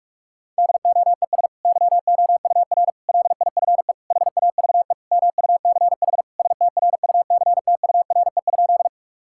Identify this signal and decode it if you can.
Morse code, heard as a keyed tone, about 46 wpm, 700 Hz. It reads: DOES YOUR LIFE HAVE MUCH STRUCTURE?
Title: Sentences from Top 1000 Words - Review Part 6 - 45wpm